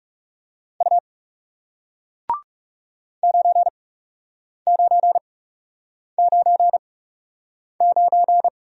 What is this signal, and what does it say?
Morse code, heard as a keyed tone, about 40 wpm, spaced as for 8 wpm, 700 Hz. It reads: U 9999